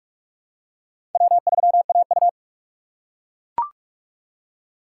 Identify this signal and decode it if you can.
Morse code, heard as a keyed tone, about 41 wpm, 700 Hz. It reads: W3AU
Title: USA Call Signs - 4 Characters - 4x Character Spacing - 45wpm